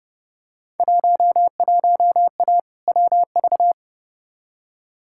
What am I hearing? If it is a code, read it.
Morse code, heard as a keyed tone, about 30 wpm, 700 Hz. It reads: 11A WV
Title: Field Day Exchanges Encore - 30wpm